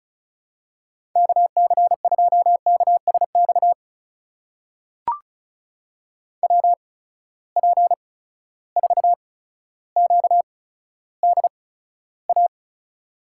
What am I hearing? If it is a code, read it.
Morse code, heard as a keyed tone, about 36 wpm, spaced as for 15 wpm, 700 Hz. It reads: KC2KSX WP4QDA
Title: USA Call Signs - 8x Character Spacing - 35wpm